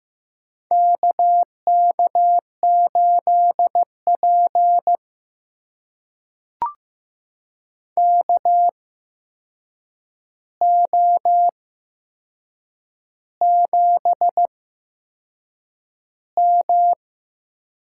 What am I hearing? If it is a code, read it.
Morse code, heard as a keyed tone, about 15 wpm, spaced as for 6 wpm, 700 Hz. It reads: KK8P KO7M
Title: USA Call Signs - 4 Characters - 8x Character Spacing - 15wpm